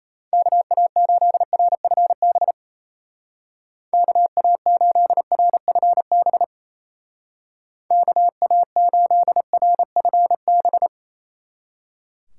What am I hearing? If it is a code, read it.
Morse code, heard as a keyed tone, about 36 wpm, 700 Hz. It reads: KA8RFB KA8RFB KA8RFB